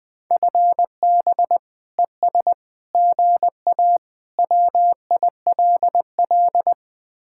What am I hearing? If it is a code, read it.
Morse code, heard as a keyed tone, about 20 wpm, 700 Hz. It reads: FB ES GA WILL
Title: Speed-Racing Common QSO Phrases - 20wpm